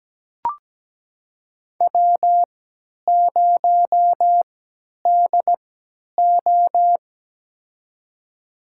Morse code, heard as a keyed tone, about 17 wpm, spaced as for 10 wpm, 700 Hz. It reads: W0DO
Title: USA Call Signs - 4 Characters - 3x Character Spacing - 17wpm